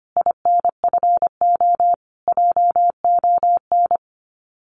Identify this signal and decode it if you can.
Morse code, heard as a keyed tone, about 25 wpm, 700 Hz. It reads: INFO JON
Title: Common QSO Phrases Encore 25wpm